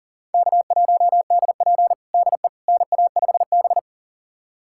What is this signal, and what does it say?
Morse code, heard as a keyed tone, about 40 wpm, 700 Hz. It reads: K1DP DE NA5B
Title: USA Callsign DE Callsign - 4 Characters - 40wpm